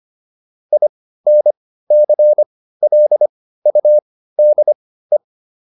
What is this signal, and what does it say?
Morse code, heard as a keyed tone, about 25 wpm, spaced as for 15 wpm, 600 Hz. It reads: INCLUDE